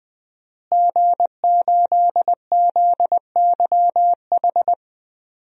Morse code, heard as a keyed tone, about 20 wpm, 700 Hz. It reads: G8ZYH
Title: DXCC-100 Call Signs - 5 Characters - Repeated 3x - 20wpm